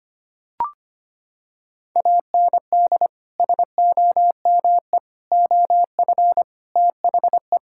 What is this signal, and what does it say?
Morse code, heard as a keyed tone, about 25 wpm, 700 Hz. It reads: AND SOME OF THE
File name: practice / 25wpm / Sentences with 4-Word Phrases - 25wpm